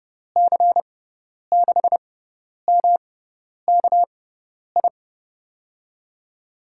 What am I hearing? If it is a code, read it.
Morse code, heard as a keyed tone, about 32 wpm, spaced as for 10 wpm, 700 Hz. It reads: C6MKI
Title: USA Call Signs - 6 Characters - 6x Character Spacing - 30wpm